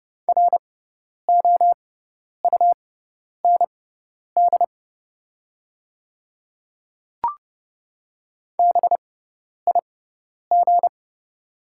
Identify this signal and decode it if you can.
Morse code, heard as a keyed tone, about 30 wpm, spaced as for 10 wpm, 700 Hz. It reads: ROUND BIG
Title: 6x Character Spacing - Top 200 Words - 30wpm